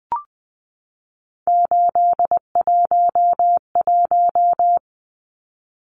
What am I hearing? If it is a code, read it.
Morse code, heard as a keyed tone, about 20 wpm, 700 Hz. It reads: 811
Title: Numbers - 3-Digit 20wpm